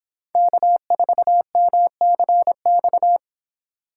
Morse code, heard as a keyed tone, about 30 wpm, 700 Hz. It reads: K4MCX